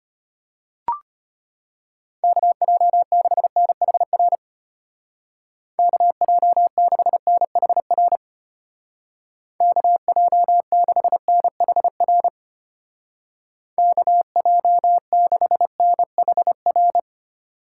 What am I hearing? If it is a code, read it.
Morse code, heard as a keyed tone, about 31 wpm, 700 Hz. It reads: KJ6NHR KJ6NHR KJ6NHR KJ6NHR
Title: Speed Racing - USA 2x3 Call Signs 25wpm